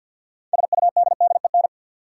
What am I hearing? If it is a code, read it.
Morse code, heard as a keyed tone, about 53 wpm, 700 Hz. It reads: SUDDEN
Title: Sets of 4 Words - New Word Drill - 50wpm